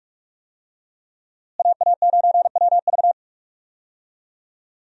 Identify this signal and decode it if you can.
Morse code, heard as a keyed tone, about 46 wpm, 700 Hz. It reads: AA9WV
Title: USA Call Signs - 5 Characters - Repeated 3x - 45wpm